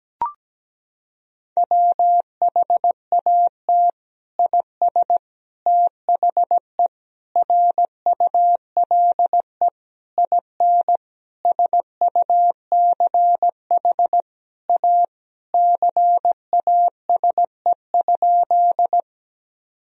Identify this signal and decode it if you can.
Morse code, heard as a keyed tone, about 17 wpm, 700 Hz. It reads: WHAT IS THE RULE IN SUCH A CASE?
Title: Sentences from Top 1000 Words - Review Part 2 - 17wpm